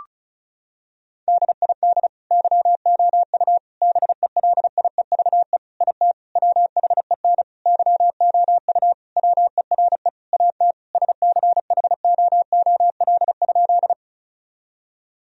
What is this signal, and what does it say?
Morse code, heard as a keyed tone, about 36 wpm, 700 Hz. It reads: DID YOU BELIEVE IT WHEN YOU WERE AT SCHOOL?
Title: Sentences from Top 1000 Words - Review Part 1 - 35wpm